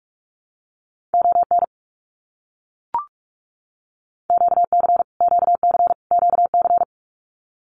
Morse code, heard as a keyed tone, about 44 wpm, 700 Hz. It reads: ON QC QC QC
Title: Canadian Provinces and Territories - Repeated 3x - 45wpm